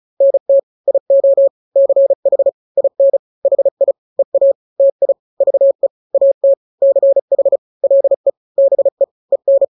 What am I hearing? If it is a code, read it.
Morse code, heard as a keyed tone, about 35 wpm, 550 Hz. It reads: NT IO CH IN HI EA TI VE AT CH LE BE EN
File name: n-grams / 2-grams 35 30 WPM